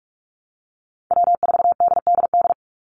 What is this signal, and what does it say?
Morse code, heard as a keyed tone, about 46 wpm, 700 Hz. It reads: W4DDD
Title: Call Signs - 45wpm